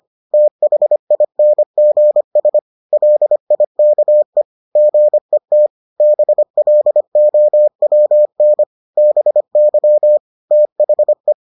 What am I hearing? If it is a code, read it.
Morse code, heard as a keyed tone, about 25 wpm, 600 Hz. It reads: THINGS LIKE GET BLOWN BY THE